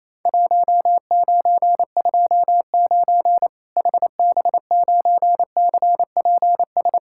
Morse code, heard as a keyed tone, about 27 wpm, 700 Hz. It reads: 1929 HB9CPS
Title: Worked All Europe DX Contest - QTC Exchange - 28wpm